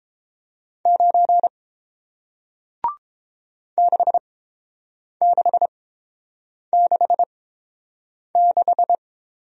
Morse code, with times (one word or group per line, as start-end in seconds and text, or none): 0.9-1.5 9
3.8-9.0 6666